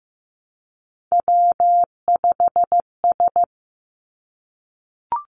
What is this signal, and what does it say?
Morse code, heard as a keyed tone, about 15 wpm, 700 Hz. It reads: W5S
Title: Three Challenging Letter-Number Combinations 15wpm